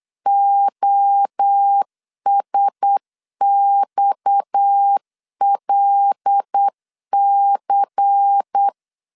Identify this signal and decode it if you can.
Morse code, heard as a keyed tone, about 8 wpm, 800 Hz. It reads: OSXLC
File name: LEZIONI / 6 PKX0 / 6 PKX0 8wpm 3341